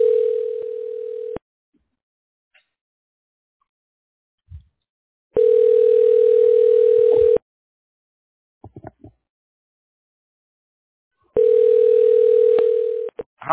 Ringback tone (North America), 440 + 480 Hz.